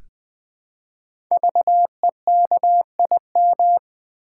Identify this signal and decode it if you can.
Morse code, heard as a keyed tone, about 20 wpm, 700 Hz. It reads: VEKIM
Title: Code Groups - 5 Letters - 20wpm